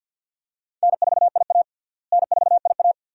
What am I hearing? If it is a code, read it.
Morse code, heard as a keyed tone, about 47 wpm, 700 Hz. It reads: N4IA N4IA
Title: USA Call Signs - 4 Characters - Repeated 2x - 50wpm